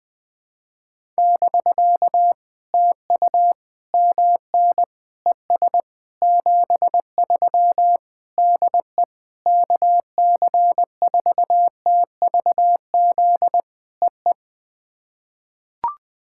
Morse code, heard as a keyed tone, about 21 wpm, 700 Hz. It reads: <BK> TU MN ES 73 DE KC4TVZ EE
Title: POTA QSO Practice - 20wpm